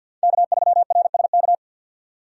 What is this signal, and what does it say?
Morse code, heard as a keyed tone, about 51 wpm, 700 Hz. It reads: K3RSK
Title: USA Call Signs - 5 Characters - 3x Character Spacing - 50wpm